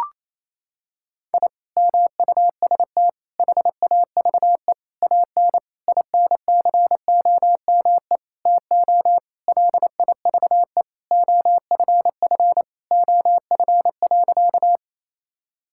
Morse code, heard as a keyed tone, about 28 wpm, 700 Hz. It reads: I MUST HAVE AN INCOME TO LIVE OFF OF.
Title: Sentences from Top 1000 Words - Review Part 5 - 28wpm